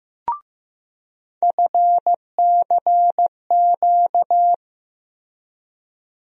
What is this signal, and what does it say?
Morse code, heard as a keyed tone, about 15 wpm, 700 Hz. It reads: FCQ
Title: USA Call Sign Postfixes - 3 Letters 15wpm